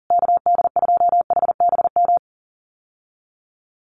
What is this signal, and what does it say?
Morse code, heard as a keyed tone, about 40 wpm, 700 Hz. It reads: KD2HBM
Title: USA Call Signs Repeated 3x 40wpm